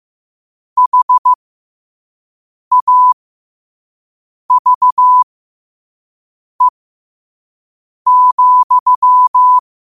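Morse code, heard as a keyed tone, about 15 wpm, spaced as for 5 wpm, 1000 Hz. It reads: HAVE,
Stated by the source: Morse quote of the day at 5 wpm.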